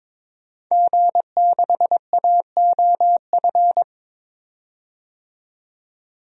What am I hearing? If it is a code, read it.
Morse code, heard as a keyed tone, about 22 wpm, 700 Hz. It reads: G6AOF
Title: DXCC-100 Call Signs - 5 Characters - 22wpm